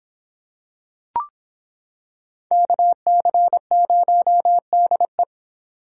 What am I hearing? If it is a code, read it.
Morse code, heard as a keyed tone, about 26 wpm, 700 Hz. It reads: KC0DE